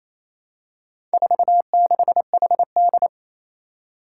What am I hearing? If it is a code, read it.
Morse code, heard as a keyed tone, about 27 wpm, 700 Hz. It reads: 46HD
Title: Four Challenging Letter-Number Combinations 28wpm